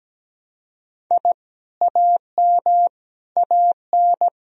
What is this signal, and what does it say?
Morse code, heard as a keyed tone, about 17 wpm, 700 Hz. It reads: I AM AN
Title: Sentences from Top 1000 Words - Review Part 1 - 17wpm